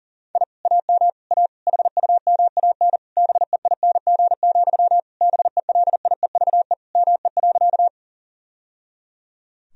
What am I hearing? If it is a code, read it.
Morse code, heard as a keyed tone, about 40 wpm, 700 Hz. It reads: I AM A HUMAN BEING, BELIEVE ME.
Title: Sentences from Top 1000 Words - Review Part 3 - 40wpm